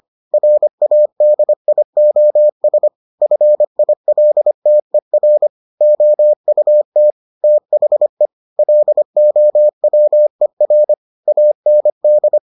Morse code, heard as a keyed tone, about 25 wpm, 600 Hz. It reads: RADIOS FILTER OUT THE LOWER AND